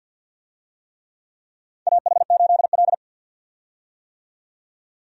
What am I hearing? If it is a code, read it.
Morse code, heard as a keyed tone, about 47 wpm, 700 Hz. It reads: AH8L
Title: USA Call Signs - 4 Characters - 50wpm